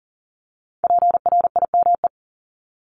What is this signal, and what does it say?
Morse code, heard as a keyed tone, about 38 wpm, 690 Hz. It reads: PRIME
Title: Multi-letter Phonemes Repeated 3x 40wpm